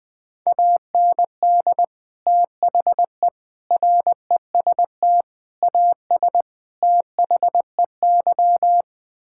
Morse code, heard as a keyed tone, about 20 wpm, 700 Hz. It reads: AND THE REST AS THEY
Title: Sets of 8 Words - 20wpm